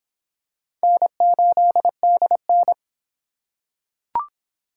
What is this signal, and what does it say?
Morse code, heard as a keyed tone, about 22 wpm, 700 Hz. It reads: N8DN